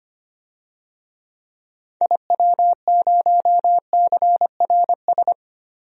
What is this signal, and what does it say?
Morse code, heard as a keyed tone, about 25 wpm, 700 Hz. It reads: IW0CRS
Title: DXCC-100 Call Signs - 6 Characters - Repeated 3x - 25wpm